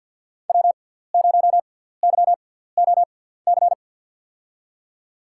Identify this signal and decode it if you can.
Morse code, heard as a keyed tone, about 52 wpm, spaced as for 17 wpm, 700 Hz. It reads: W0YOC
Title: USA Call Signs - 6x Character Spacing - 50wpm